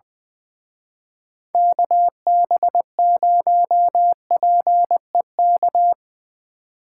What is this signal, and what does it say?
Morse code, heard as a keyed tone, about 23 wpm, 700 Hz. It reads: KB0PEK
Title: Speed Racing - USA Call Signs 20wpm